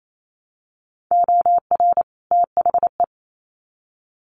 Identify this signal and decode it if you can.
Morse code, heard as a keyed tone, about 28 wpm, 700 Hz. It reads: OR THE
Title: Sets of 2 Words 28wpm